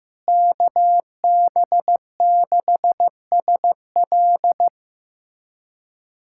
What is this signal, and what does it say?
Morse code, heard as a keyed tone, about 15 wpm, 700 Hz. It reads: KB6SL
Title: USA Callsign DE Callsign 15wpm